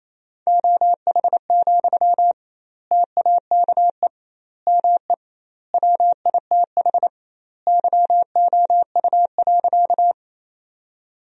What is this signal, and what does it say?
Morse code, heard as a keyed tone, about 28 wpm, 700 Hz. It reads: OH, TAKE ME WITH YOU.
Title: 2x Word-Spacing - Sentences from Top 500 Words - 28wpm